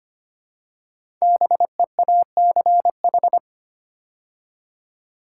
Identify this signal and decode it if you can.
Morse code, heard as a keyed tone, about 25 wpm, 700 Hz. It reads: BEACH
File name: 5 Letter Words - First 3 Letters Given - 25wpm